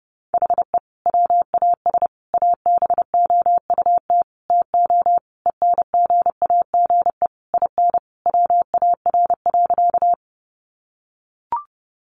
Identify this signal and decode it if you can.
Morse code, heard as a keyed tone, about 30 wpm, 700 Hz. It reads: HE WAS ABOUT TO ENGAGE IN WAR.
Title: Sentences from Top 1500 Words 30wpm